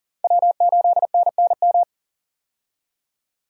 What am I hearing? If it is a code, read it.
Morse code, heard as a keyed tone, about 40 wpm, 700 Hz. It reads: W8NNM
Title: USA Call Signs - 5 Characters - 40wpm